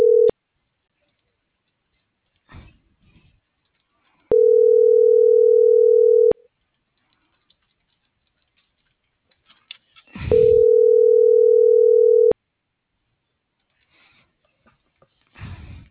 Ringback tone (North America), 440 + 480 Hz.